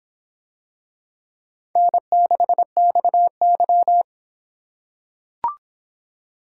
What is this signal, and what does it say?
Morse code, heard as a keyed tone, about 24 wpm, 700 Hz. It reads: N6XY